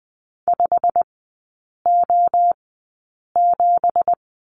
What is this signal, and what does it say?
Morse code, heard as a keyed tone, about 19 wpm, spaced as for 8 wpm, 690 Hz. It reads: 5O7
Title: 166 Lesson 27e - Number 7 - Three Characters 20wpm